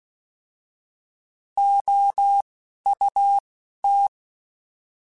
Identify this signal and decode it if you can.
Morse code, heard as a keyed tone, about 16 wpm, spaced as for 11 wpm, 800 Hz. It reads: OUT
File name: Alphabetical / Alphabetical mp3s / Os @10